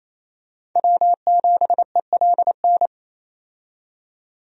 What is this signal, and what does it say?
Morse code, heard as a keyed tone, about 28 wpm, 700 Hz. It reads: W7ELN
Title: USA Call Signs - 5 Characters - 28wpm